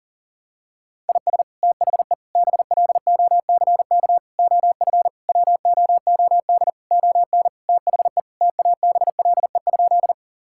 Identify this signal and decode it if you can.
Morse code, heard as a keyed tone, about 40 wpm, 700 Hz. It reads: IS THE BLOCK OF WOOD ON THE TABLE?
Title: Sentences from Top 1600 Words 40wpm